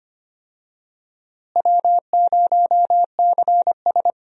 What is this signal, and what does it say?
Morse code, heard as a keyed tone, about 25 wpm, 700 Hz. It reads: W0CS
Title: USA Call Signs - 4 Characters - Repeated 2x - 25wpm